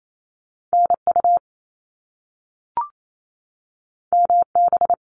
Morse code, heard as a keyed tone, about 28 wpm, 700 Hz. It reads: NU MB
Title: Canadian Provinces and Territories 28wpm